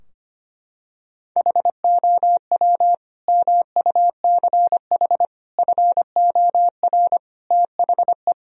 Morse code, heard as a keyed tone, about 25 wpm, 700 Hz. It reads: HOW MUCH FOR THE
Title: Sentences from Top 1000 Words - Review Part 3 - 25wpm